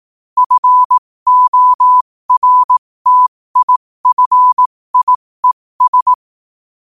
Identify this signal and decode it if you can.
Morse code, heard as a keyed tone, about 18 wpm, spaced as for 15 wpm, 1000 Hz. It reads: FORTIFIES
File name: qotdfiles / 15wpm